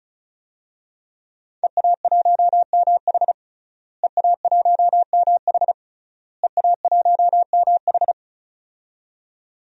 Morse code, heard as a keyed tone, about 36 wpm, 700 Hz. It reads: EA1MH EA1MH EA1MH
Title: DXCC-100 Call Signs - 5 Characters - Repeated 3x - 35wpm